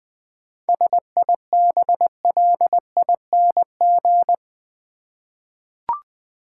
Morse code, heard as a keyed tone, about 21 wpm, 700 Hz. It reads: SIBLING
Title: 7 Letter Words - First 3 Letters Given - 20wpm